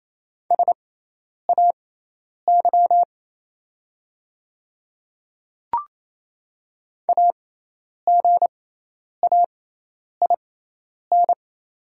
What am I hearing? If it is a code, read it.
Morse code, heard as a keyed tone, about 27 wpm, spaced as for 10 wpm, 700 Hz. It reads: SAY AGAIN
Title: 6x Character Spacing - Top 200 Words - 28wpm